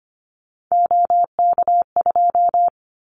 Morse code, heard as a keyed tone, about 25 wpm, 700 Hz. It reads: OK2
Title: Callsign Prefixes for Top 25 DXCC - Repeated 3x - 25wpm